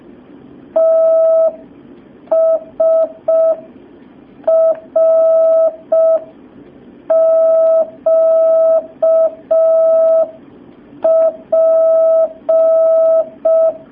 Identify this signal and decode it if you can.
Morse code, heard as a keyed tone, about 5 wpm, 650 Hz. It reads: TSRQP
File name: jh8cbh / 2019cw